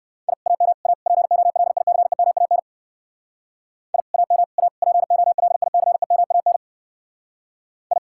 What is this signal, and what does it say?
Morse code, heard as a keyed tone, about 59 wpm, 700 Hz. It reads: I AM A POLICEMAN I AM A POLICEMAN I